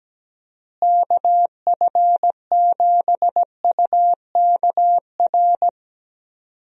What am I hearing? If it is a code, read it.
Morse code, heard as a keyed tone, about 17 wpm, 700 Hz. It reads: KF7UKR
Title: USA Call Signs - 6 Characters - Repeated 2x - 17wpm